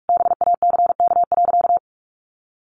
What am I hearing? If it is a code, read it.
Morse code, heard as a keyed tone, about 44 wpm, 700 Hz. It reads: BACK.
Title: Sentences from Top 1300 Words 45wpm